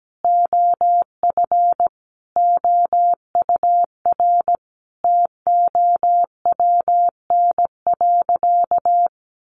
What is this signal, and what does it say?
Morse code, heard as a keyed tone, about 17 wpm, 700 Hz. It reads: OF OUR TOWN.
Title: Sentences from Top 1500 Words 17wpm